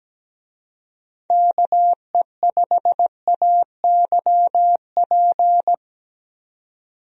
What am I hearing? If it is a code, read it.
Morse code, heard as a keyed tone, about 17 wpm, 700 Hz. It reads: KE5AYP
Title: USA Call Signs - 6 Characters - 17wpm